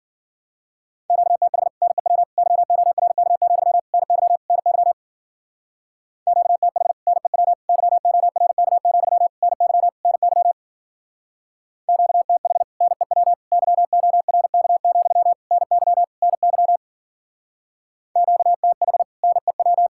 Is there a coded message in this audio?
Yes: Morse code from 1.1 to 19.9 s, heard as a keyed tone, about 50 wpm, 700 Hz. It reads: QTH NEW YORK, NY NY QTH NEW YORK, NY NY QTH NEW YORK, NY NY QTH NEW